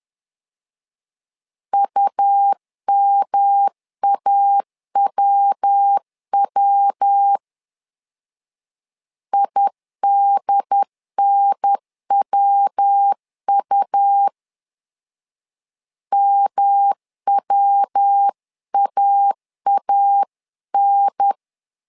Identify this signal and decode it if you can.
Morse code, heard as a keyed tone, about 10 wpm, 800 Hz. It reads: UMAWW IDNWU MWAAN
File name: LEZIONI / 3 MWRO / 3 MW 10wpm 1959